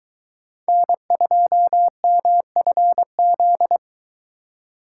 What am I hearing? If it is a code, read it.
Morse code, heard as a keyed tone, about 23 wpm, 700 Hz. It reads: N2MFZ